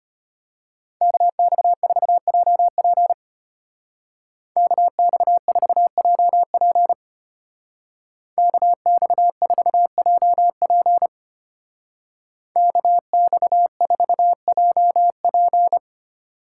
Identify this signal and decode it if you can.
Morse code, heard as a keyed tone, about 31 wpm, 700 Hz. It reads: KX4JP KX4JP KX4JP KX4JP
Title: Speed Racing - USA Call Signs 25wpm